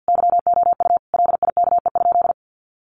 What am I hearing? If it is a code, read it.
Morse code, heard as a keyed tone, about 50 wpm, 700 Hz. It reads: YOU LIKE?
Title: Sentences from Top 1200 Words 50wpm